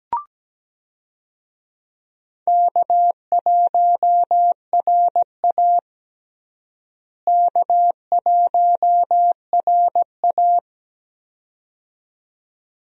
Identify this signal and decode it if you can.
Morse code, heard as a keyed tone, about 17 wpm, 700 Hz. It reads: K1RA K1RA
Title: USA Call Signs - 4 Characters - Repeated 2x - 17wpm